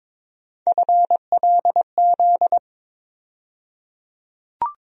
Morse code, heard as a keyed tone, about 22 wpm, 700 Hz. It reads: FLZ
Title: USA Call Sign Postfixes - 3 Letters 22wpm